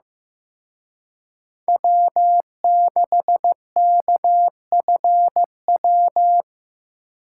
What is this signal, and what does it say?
Morse code, heard as a keyed tone, about 18 wpm, 700 Hz. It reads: W6KFW
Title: Speed Racing - USA 1x3 Call Signs 15wpm